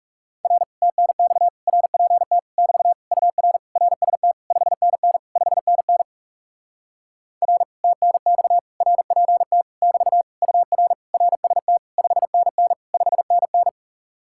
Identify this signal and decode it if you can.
Morse code, heard as a keyed tone, about 37 wpm, 700 Hz. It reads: R TNX RPT <BT> UR RST 5NN 5NN R TNX RPT <BT> UR RST 5NN 5NN
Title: Speed-Racing Common QSO Phrases - 30wpm